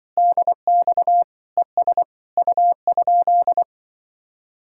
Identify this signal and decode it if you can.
Morse code, heard as a keyed tone, about 24 wpm, 700 Hz. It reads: DX ES U?
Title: Speed-Racing Common QSO Phrases - 20wpm